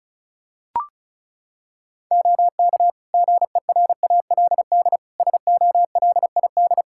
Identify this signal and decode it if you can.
Morse code, heard as a keyed tone, about 34 wpm, 700 Hz. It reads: OK GERALD SOLID
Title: Practice QSO - Exchange 3 - More Info - 35wpm